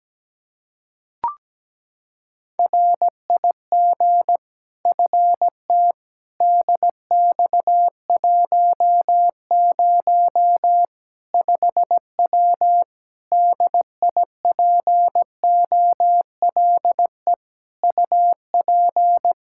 Morse code, heard as a keyed tone, about 17 wpm, 700 Hz. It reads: RIG FT DX10 5W DIPOLE UP
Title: Practice QSO - Exchange 2 - Info - 17wpm